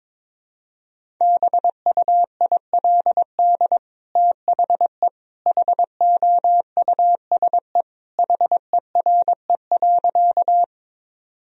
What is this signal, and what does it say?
Morse code, heard as a keyed tone, about 21 wpm, 700 Hz. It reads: BUILD THE HOUSE HERE.
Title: Sentences from Top 1000 Words - Review Part 1 - 22wpm